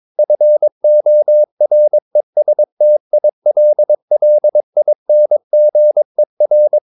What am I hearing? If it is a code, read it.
Morse code, heard as a keyed tone, about 22 wpm, 600 Hz. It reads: FORESTILLINGER
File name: Ferie QRS 22wpm